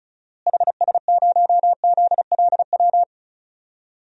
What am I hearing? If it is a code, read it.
Morse code, heard as a keyed tone, about 36 wpm, 700 Hz. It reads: HS0ZLW
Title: DXCC-100 Call Signs - 6 Characters - Repeated 2x - 35wpm